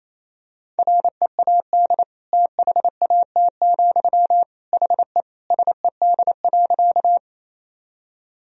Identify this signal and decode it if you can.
Morse code, heard as a keyed tone, about 28 wpm, 700 Hz. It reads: READ THAT, HE SED.
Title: Sentences from Top 200 Words and Abbreviations 28wpm